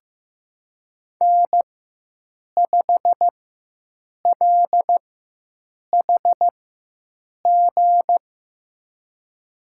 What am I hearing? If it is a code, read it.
Morse code, heard as a keyed tone, about 15 wpm, spaced as for 7 wpm, 700 Hz. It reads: N5LHG
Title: USA Call Signs - 4x Character Spacing - 15wpm